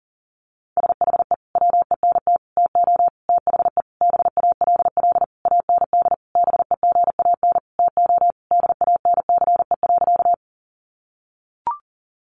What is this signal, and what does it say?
Morse code, heard as a keyed tone, about 37 wpm, 700 Hz. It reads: SHE WENT TO THE BALL AND BEGAN TO DANCE.
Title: Sentences from Top 900 Words 40wpm